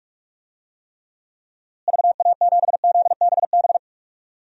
Morse code, heard as a keyed tone, about 46 wpm, 700 Hz. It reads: VA7ZBB
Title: DXCC-100 Call Signs - 6 Characters - Repeated 2x - 45wpm